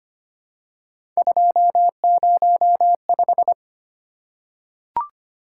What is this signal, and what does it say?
Morse code, heard as a keyed tone, about 25 wpm, 700 Hz. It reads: 205
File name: Three Letter-Number Combinations - 25wpm